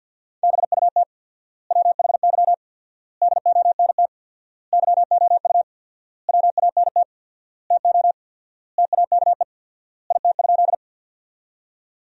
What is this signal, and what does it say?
Morse code, heard as a keyed tone, about 48 wpm, 700 Hz. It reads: BUT WHY DONT YOU WANT TO TAKE IT?
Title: 4x Word-Spacing - Sentences from Top 300 Words - 50wpm